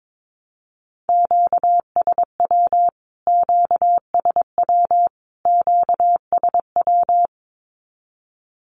Morse code, heard as a keyed tone, about 22 wpm, 700 Hz. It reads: QSW QSW QSW
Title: Q-Codes Repeated 3x 22wpm